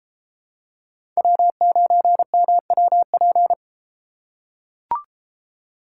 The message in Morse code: W9MWP

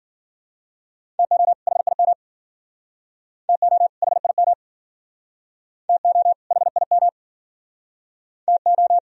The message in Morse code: TO HIM TO HIM TO HIM TO

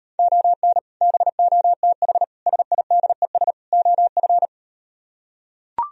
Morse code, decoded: ON BOTH SIDES OF

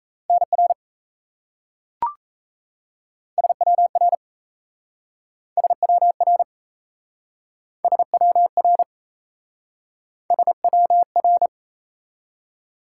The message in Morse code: NR SWR SWR SWR SWR

